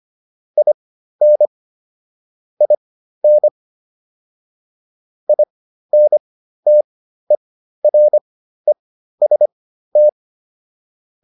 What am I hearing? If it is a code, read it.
Morse code, heard as a keyed tone, about 25 wpm, spaced as for 13 wpm, 600 Hz. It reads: IN IN INTEREST